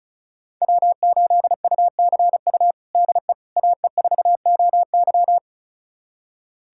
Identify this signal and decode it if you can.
Morse code, heard as a keyed tone, about 36 wpm, 700 Hz. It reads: W8UCU DE AE4OY
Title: USA Callsign DE Callsign 35wpm